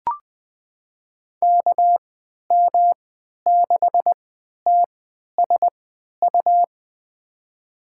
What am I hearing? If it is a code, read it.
Morse code, heard as a keyed tone, about 20 wpm, spaced as for 11 wpm, 700 Hz. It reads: KM6TSU
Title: USA Call Signs - 6 Characters - 3x Character Spacing - 20wpm